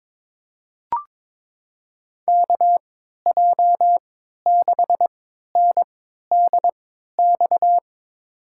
Morse code, heard as a keyed tone, about 22 wpm, spaced as for 12 wpm, 700 Hz. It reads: KJ6NDX
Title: USA Call Signs - 6 Characters - 3x Character Spacing - 22wpm